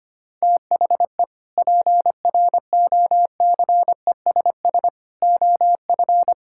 Morse code, heard as a keyed tone, about 25 wpm, 700 Hz. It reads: THE PROCESS OF
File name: practice / 25wpm / Sentences with 3-Word Phrases - 25wpm